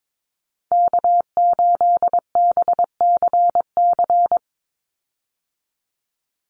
Morse code, heard as a keyed tone, about 22 wpm, 700 Hz. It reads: K8BCC
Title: USA Call Signs 22wpm